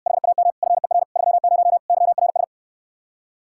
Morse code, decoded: HAM FER 30 YRS

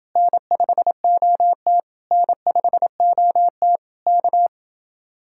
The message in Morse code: N5OT N5OT K

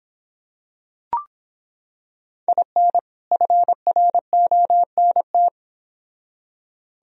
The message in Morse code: IN FRONT

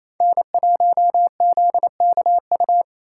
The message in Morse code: N1ZKU

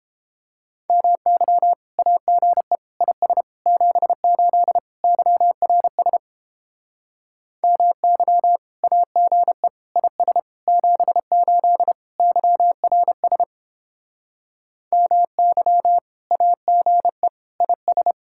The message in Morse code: MY AGE IS 78 YRS MY AGE IS 78 YRS MY AGE IS

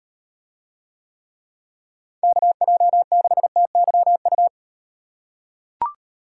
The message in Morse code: KJ6TYU

